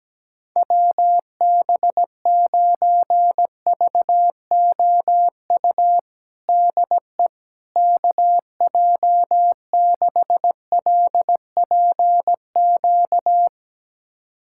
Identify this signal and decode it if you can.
Morse code, heard as a keyed tone, about 17 wpm, 700 Hz. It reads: WB9VOU DE KJ6LPQ
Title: USA Callsign DE Callsign - 6 Characters - 17wpm